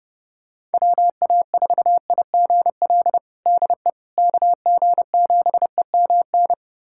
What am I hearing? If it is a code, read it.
Morse code, heard as a keyed tone, about 32 wpm, 700 Hz. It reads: WA4IGL DE KG7EMN
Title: USA Callsign DE Callsign - 6 Characters - 30wpm